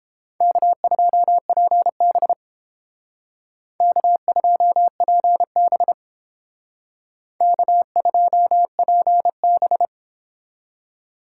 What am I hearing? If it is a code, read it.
Morse code, heard as a keyed tone, about 31 wpm, 700 Hz. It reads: K2PB K2PB K2PB